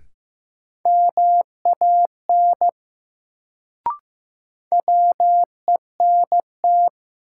Morse code, heard as a keyed tone, about 15 wpm, 700 Hz. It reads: MAN WENT
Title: Sentences from Top 200 Words - New Word Drill - 15wpm